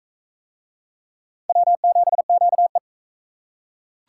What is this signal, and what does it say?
Morse code, heard as a keyed tone, about 43 wpm, 700 Hz. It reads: W7QE